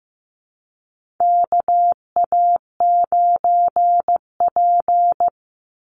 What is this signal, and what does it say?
Morse code, heard as a keyed tone, about 15 wpm, 700 Hz. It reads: KA9P
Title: Call Signs Repeated 2x - 15wpm